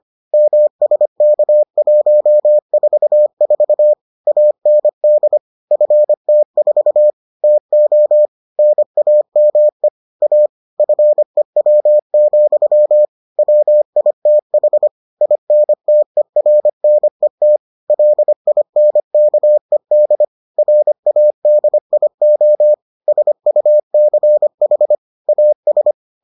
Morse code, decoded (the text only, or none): MSK144 AND FT4 TO NAME A FEW, WITH INTERNET LINKED RADIO SUCH AS